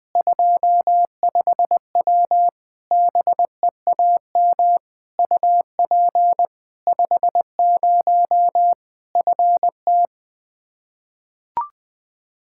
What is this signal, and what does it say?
Morse code, heard as a keyed tone, about 20 wpm, 700 Hz. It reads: 25W BEAM UP 50 FT
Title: Practice QSO - Exchange 2 - Info - 20wpm